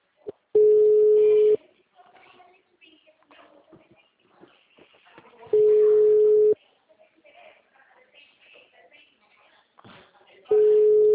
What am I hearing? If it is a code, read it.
Ringback tone (Europe, 425 Hz), with speech.
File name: out- -2003- - - .1441